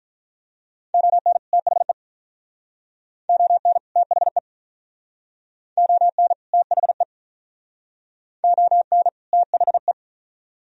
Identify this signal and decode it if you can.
Morse code, heard as a keyed tone, about 45 wpm, 700 Hz. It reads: ON THE ON THE ON THE ON THE